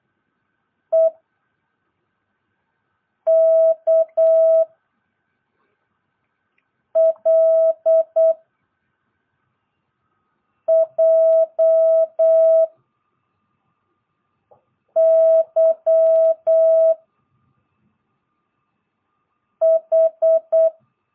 Morse code, heard as a keyed tone, about 8 wpm, 660 Hz. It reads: E K L J Y H
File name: jh8cbh / 2022cw